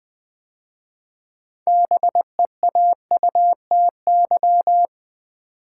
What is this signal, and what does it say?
Morse code, heard as a keyed tone, about 20 wpm, 700 Hz. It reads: BEAUTY
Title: 6 Letter Words - First 3 Letters Given - 20wpm